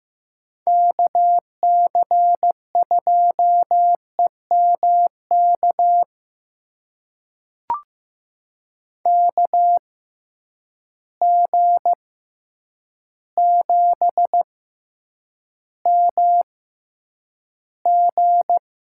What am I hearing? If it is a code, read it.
Morse code, heard as a keyed tone, about 15 wpm, spaced as for 8 wpm, 700 Hz. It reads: KC2EMK KG7MG
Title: USA Call Signs - 6 Characters - 6x Character Spacing - 15wpm